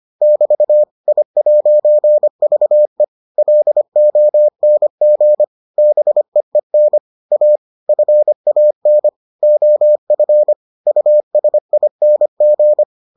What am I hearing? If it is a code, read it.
Morse code, heard as a keyed tone, about 25 wpm, 600 Hz. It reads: <BT> I'VE LONG BEEN A FAN OF USING